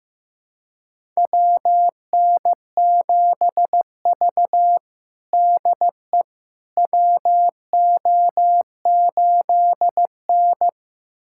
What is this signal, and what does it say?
Morse code, heard as a keyed tone, about 16 wpm, 700 Hz. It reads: WN7V DE WO8N
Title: USA Callsign DE Callsign - 4 Characters - 15wpm